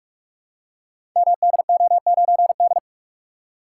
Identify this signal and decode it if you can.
Morse code, heard as a keyed tone, about 44 wpm, 700 Hz. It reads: MDO9D
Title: Code Groups - 5 Letter:Numbers - 45wpm